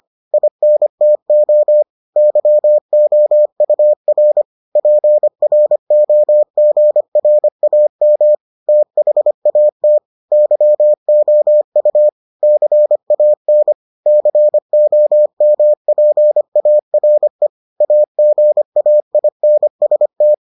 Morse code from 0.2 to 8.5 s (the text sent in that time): INTO YOUR PROGRAM